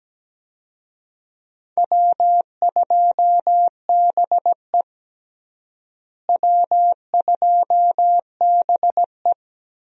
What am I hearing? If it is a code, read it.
Morse code, heard as a keyed tone, about 17 wpm, 700 Hz. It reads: W2BE W2BE
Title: USA Call Signs - 4 Characters - Repeated 2x - 17wpm